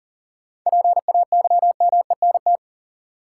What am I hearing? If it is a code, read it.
Morse code, heard as a keyed tone, about 40 wpm, 700 Hz. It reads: PAYMENT
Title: Sentences from Top 1200 Words - New Word Drill - 40wpm